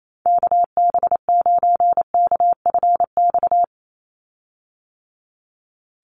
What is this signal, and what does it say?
Morse code, heard as a keyed tone, about 28 wpm, 700 Hz. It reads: KB9KFX
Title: USA Call Signs 28wpm